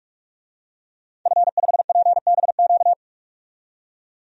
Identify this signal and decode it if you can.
Morse code, heard as a keyed tone, about 46 wpm, 700 Hz. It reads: F5PBQ
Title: DXCC-100 Call Signs - 5 Characters - 45wpm